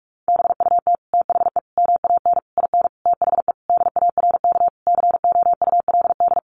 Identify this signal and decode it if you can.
Morse code, heard as a keyed tone, about 44 wpm, 700 Hz. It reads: BUT THE MAN IN THE DARK COULD
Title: Sentences from Top 600 Words 45wpm